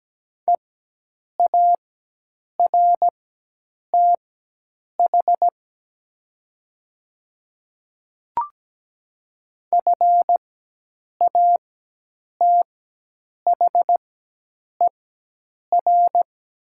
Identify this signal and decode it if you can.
Morse code, heard as a keyed tone, about 17 wpm, spaced as for 8 wpm, 700 Hz. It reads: EARTH FATHER
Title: 4x Character Spacing - Top 200 Words - 17wpm